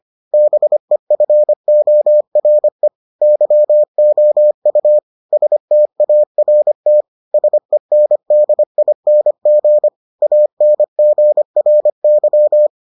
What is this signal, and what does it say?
Morse code, heard as a keyed tone, about 25 wpm, 600 Hz. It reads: BEFORE YOU START SENDING ANGRY